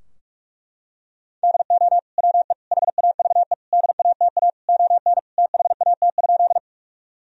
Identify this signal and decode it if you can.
Morse code, heard as a keyed tone, about 41 wpm, 700 Hz. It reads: DO WE HAVE DATA ON THAT?
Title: Sentences from Top 1000 Words - Review Part 3 - 45wpm